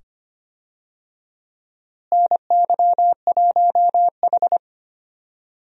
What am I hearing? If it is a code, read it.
Morse code, heard as a keyed tone, about 25 wpm, 700 Hz. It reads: NY1H
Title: USA Call Signs - 4 Characters - Repeated 2x - 25wpm